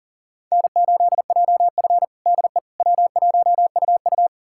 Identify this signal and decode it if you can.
Morse code, heard as a keyed tone, about 40 wpm, 700 Hz. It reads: N8JF DE W1UU
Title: USA Callsign DE Callsign - 4 Characters - 40wpm